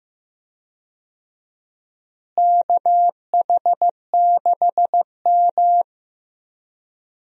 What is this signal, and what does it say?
Morse code, heard as a keyed tone, about 16 wpm, 700 Hz. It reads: KH6M
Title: USA Call Signs - 4 Characters - Repeated 2x - 15wpm